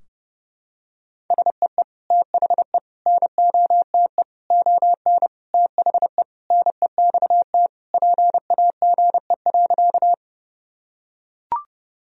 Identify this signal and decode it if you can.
Morse code, heard as a keyed tone, about 30 wpm, 700 Hz. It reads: SEE THE NOTE ON THE NEXT PAGE.
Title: Sentences from Top 1000 Words - Review Part 2 - 30wpm